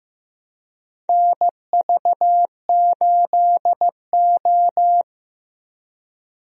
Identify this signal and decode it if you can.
Morse code, heard as a keyed tone, about 15 wpm, 700 Hz. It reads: NV8O
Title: USA Call Signs - 4 Characters - Repeated 3x - 15wpm